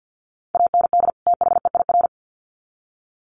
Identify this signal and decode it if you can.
Morse code, heard as a keyed tone, about 50 wpm, 700 Hz. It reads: AND THEIR
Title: Sets of 2 Words 50wpm